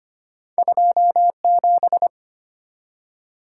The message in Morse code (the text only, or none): 27